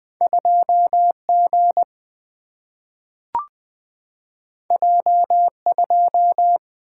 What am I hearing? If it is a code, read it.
Morse code, heard as a keyed tone, about 20 wpm, 700 Hz. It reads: 2G J2
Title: Single-Letter Single-Number 20wpm